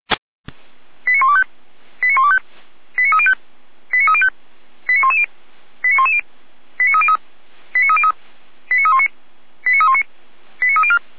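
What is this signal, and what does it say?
Five-tone selective call (ZVEI), heard as quick runs of five tones, tones of about 70 ms. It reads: 89125 89125 89305 89305 89110 89110 89303 89303 89319 89319 89304
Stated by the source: Received by SDR, ZVEI five-tone call 89294.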